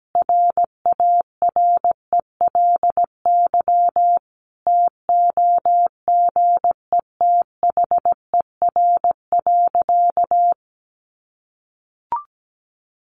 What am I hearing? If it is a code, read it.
Morse code, heard as a keyed tone, about 17 wpm, 700 Hz. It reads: RARELY TOGETHER.
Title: Sentences from Top 1400 Words 17wpm